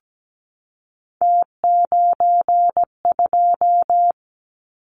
Morse code, heard as a keyed tone, about 17 wpm, 700 Hz. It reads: T92
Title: Callsign Prefixes for Top 50 DXCC - 17wpm